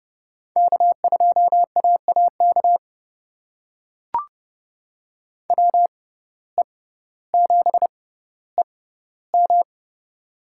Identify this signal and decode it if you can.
Morse code, heard as a keyed tone, about 30 wpm, spaced as for 15 wpm, 700 Hz. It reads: K2AAK WE7EM
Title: USA Call Signs - 5 Characters - 6x Character Spacing - 30wpm